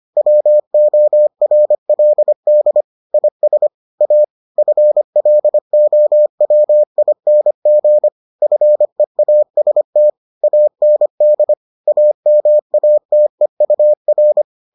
Morse code, heard as a keyed tone, about 25 wpm, 600 Hz. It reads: WORLD IS A FLOWING FEAST AND AMATEUR